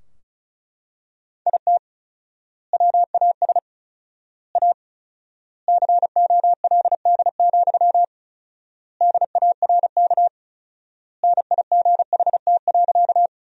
Morse code, heard as a keyed tone, about 34 wpm, 700 Hz. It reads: IT WAS A COLD, DARK NIGHT.
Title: 4x Word-Spacing - Sentences from Top 500 Words - 35wpm